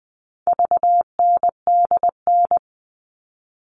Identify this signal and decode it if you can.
Morse code, heard as a keyed tone, about 20 wpm, 700 Hz. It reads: VNDN
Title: All Cut Numbers - 4-Digit 20wpm